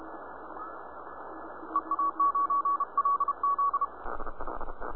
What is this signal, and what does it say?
Morse code, heard as a keyed tone, about 29 wpm, 1130 Hz, with noise. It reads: EA9LZ
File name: ct1boh / flare